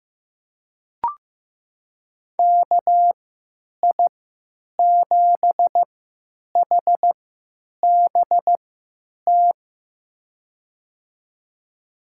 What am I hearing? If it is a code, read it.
Morse code, heard as a keyed tone, about 15 wpm, spaced as for 9 wpm, 700 Hz. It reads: KI7HBT
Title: USA Call Signs - 6 Characters - 3x Character Spacing - 15wpm